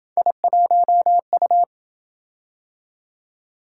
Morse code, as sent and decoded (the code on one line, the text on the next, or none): .. .---- ..-
I1U